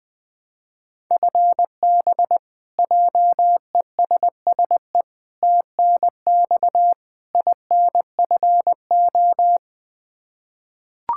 Morse code, heard as a keyed tone, about 20 wpm, 700 Hz. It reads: FB JESSE TNX INFO
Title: K1USN SST QSO Practice 20wpm